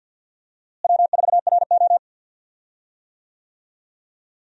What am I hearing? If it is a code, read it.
Morse code, heard as a keyed tone, about 52 wpm, 700 Hz. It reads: W4RO
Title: USA Call Signs - 4 Characters - Repeated 2x - 50wpm